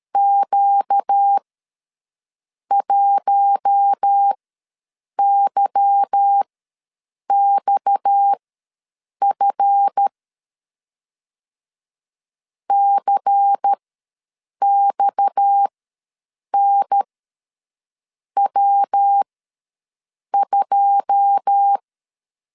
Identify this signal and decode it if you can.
Morse code, heard as a keyed tone, about 13 wpm, spaced as for 6 wpm, 800 Hz. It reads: Q1YXF CXNW2
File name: LEZIONI / 9 286 ws 12wpm 3351